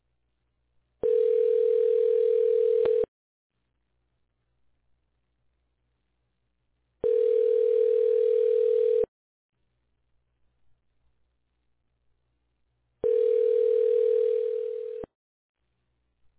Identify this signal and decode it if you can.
Ringback tone (North America), 440 + 480 Hz.